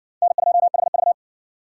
Morse code, heard as a keyed tone, about 59 wpm, 700 Hz. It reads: N2HV